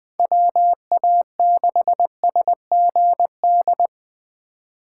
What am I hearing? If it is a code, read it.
Morse code, heard as a keyed tone, about 23 wpm, 700 Hz. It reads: WA6SGD
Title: Speed Racing - USA Call Signs 20wpm